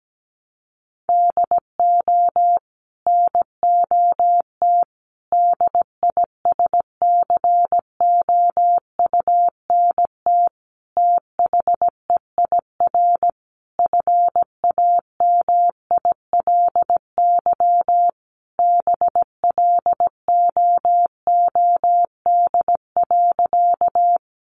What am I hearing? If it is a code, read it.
Morse code, heard as a keyed tone, about 17 wpm, 700 Hz. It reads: DO NOT DISCOUNT THEIR FAMILY BLOOD.
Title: Sentences from Top 1200 Words 17wpm